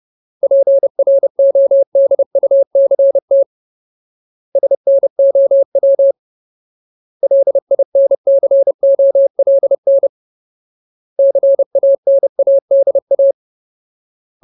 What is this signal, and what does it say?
Morse code, heard as a keyed tone, about 30 wpm, 550 Hz. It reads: PRODUCT SNOW LINCOLN CANADA
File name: Ultimate Word List / 30WPM Ultimate